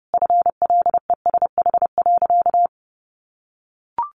Morse code, heard as a keyed tone, about 30 wpm, 700 Hz. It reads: FLESH.
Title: A Sound Of Thunder 30wpm